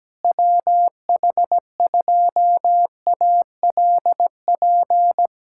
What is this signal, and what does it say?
Morse code, heard as a keyed tone, about 17 wpm, 700 Hz. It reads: WH2ALP
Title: USA Callsign DE Callsign - 6 Characters - 17wpm